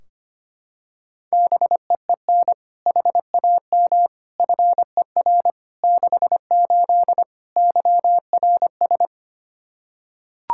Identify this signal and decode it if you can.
Morse code, heard as a keyed tone, about 25 wpm, 700 Hz. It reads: BEEN HAM FER 68 YRS
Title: Practice QSO - Exchange 3 - More Info - 25wpm